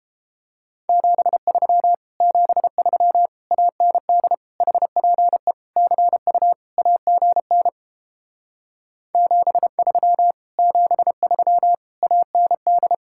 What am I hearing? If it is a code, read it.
Morse code, heard as a keyed tone, about 29 wpm, 700 Hz. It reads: 73 73 AND HPE CU AGN 73 73 AND